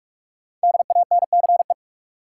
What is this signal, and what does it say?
Morse code, heard as a keyed tone, about 46 wpm, 700 Hz. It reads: DANCE
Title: Sentences from Top 700 Words - New Word Drill - 45wpm